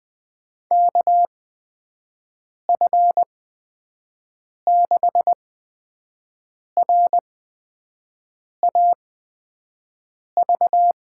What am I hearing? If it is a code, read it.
Morse code, heard as a keyed tone, about 20 wpm, spaced as for 5 wpm, 700 Hz. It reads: KF6RAV
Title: USA Call Signs - 6 Characters - 8x Character Spacing - 20wpm